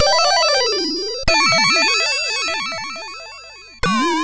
There is no code.